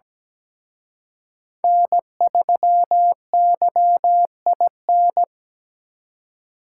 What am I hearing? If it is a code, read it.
Morse code, heard as a keyed tone, about 20 wpm, 700 Hz. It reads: N3YIN
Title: Speed Racing - USA 1x3 Call Signs 17wpm